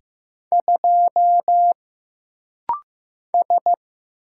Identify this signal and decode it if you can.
Morse code, heard as a keyed tone, about 15 wpm, 700 Hz. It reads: 2 S
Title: Single Letter-Number Alt - Rapid-Fire 15wpm